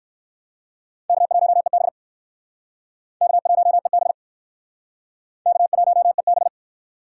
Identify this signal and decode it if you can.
Morse code, heard as a keyed tone, about 61 wpm, 700 Hz. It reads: K1EB K1EB K1EB